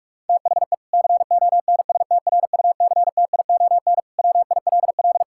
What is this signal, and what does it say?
Morse code, heard as a keyed tone, about 46 wpm, 700 Hz. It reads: THE CONSTRUCTION WILL
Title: Sentences from Top 1600 Words 45wpm